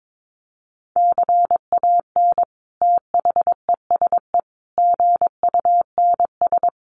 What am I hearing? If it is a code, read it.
Morse code, heard as a keyed tone, about 22 wpm, 700 Hz. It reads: CAN THESE GUNS
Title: A Sound Of Thunder 22wpm